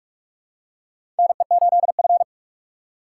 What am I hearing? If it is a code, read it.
Morse code, heard as a keyed tone, about 46 wpm, 700 Hz. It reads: NE8F
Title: USA Call Signs - 4 Characters - 6x Character Spacing - 45wpm